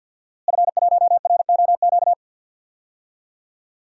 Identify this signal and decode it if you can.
Morse code, heard as a keyed tone, about 52 wpm, 700 Hz. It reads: F1ROQ
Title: DXCC-100 Call Signs - 5 Characters - Repeated 3x - 50wpm